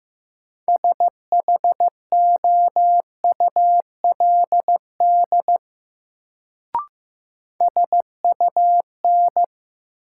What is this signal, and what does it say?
Morse code, heard as a keyed tone, about 16 wpm, 700 Hz. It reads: SHOULD SUN
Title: Sentences from Top 300 Words - New Word Drill - 15wpm